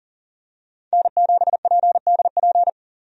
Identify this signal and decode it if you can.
Morse code, heard as a keyed tone, about 37 wpm, 700 Hz. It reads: N7PDP
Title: USA Call Signs - 5 Characters - 40wpm